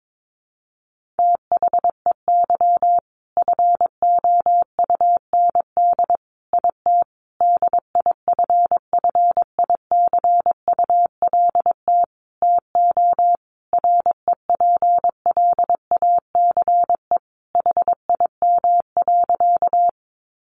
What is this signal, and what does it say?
Morse code, heard as a keyed tone, about 22 wpm, 700 Hz. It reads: THEY FOUND IT DIFFICULT TO REPLACE HIM.
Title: Sentences from Top 1400 Words 22wpm